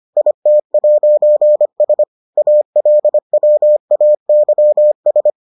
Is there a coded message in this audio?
Yes: Morse code from 0.2 to 5.3 s, heard as a keyed tone, about 25 wpm, 600 Hz. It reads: IT'S ALWAYS